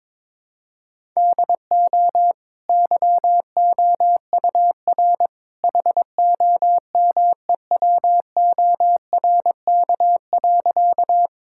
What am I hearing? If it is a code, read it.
Morse code, heard as a keyed tone, about 21 wpm, 690 Hz. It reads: DO YOUR HOMEWORK.
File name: practice / 22wpm / Easy Sentences with Hints - 22wpm